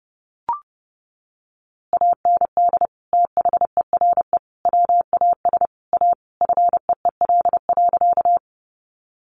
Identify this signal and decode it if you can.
Morse code, heard as a keyed tone, about 30 wpm, 700 Hz. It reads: AND THERE WAS A FEEL.
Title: A Sound Of Thunder 30wpm